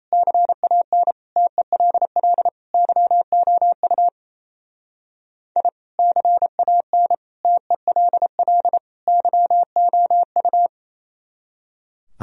Morse code, heard as a keyed tone, about 35 wpm, 700 Hz. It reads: CAN TELL YOU I CAN TELL YOU
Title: Speed-Racing - Sets of 4 Words 28wpm